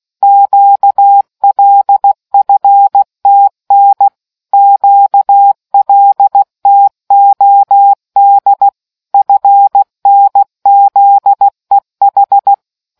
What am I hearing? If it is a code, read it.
Morse code, heard as a keyed tone, about 16 wpm, 800 Hz. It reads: QLFTN QLTOD FNZEH